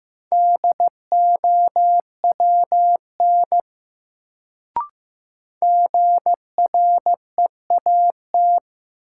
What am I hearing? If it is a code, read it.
Morse code, heard as a keyed tone, about 16 wpm, 700 Hz. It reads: DOWN GREAT
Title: Sets of 2 Words - New Word Drill - 15wpm